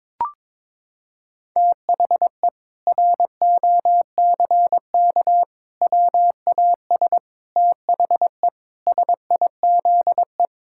Morse code, heard as a keyed tone, about 22 wpm, 700 Hz. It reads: THE ROCK WAS THE SIZE
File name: practice / 22wpm / Sentences with 4-Word Phrases - 22wpm